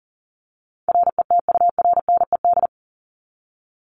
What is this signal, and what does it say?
Morse code, heard as a keyed tone, about 40 wpm, 700 Hz. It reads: RETURNED
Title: ED Suffix - Top 100 Words with Base Words 40wpm